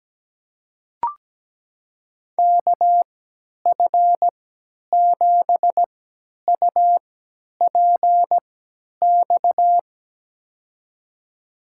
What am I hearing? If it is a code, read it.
Morse code, heard as a keyed tone, about 17 wpm, spaced as for 10 wpm, 700 Hz. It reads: KF7UPX
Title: USA Call Signs - 6 Characters - 3x Character Spacing - 17wpm